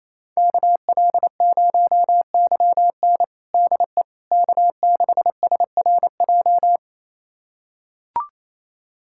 Morse code, heard as a keyed tone, about 28 wpm, 700 Hz. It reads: KL0YN DE K6SRJ
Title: USA Callsign DE Callsign - 5 Characters - 28wpm